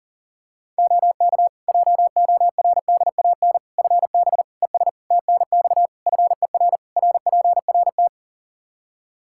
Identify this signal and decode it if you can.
Morse code, heard as a keyed tone, about 40 wpm, 700 Hz. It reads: OK JORDAN FB ES TNX FER RPRT
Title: Practice QSO - Exchange 2 - Info - 40wpm